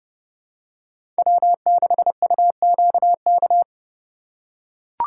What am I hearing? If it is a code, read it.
Morse code, heard as a keyed tone, about 30 wpm, 700 Hz. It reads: W6UQK
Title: USA Call Signs - 5 Characters - 4x Character Spacing - 30wpm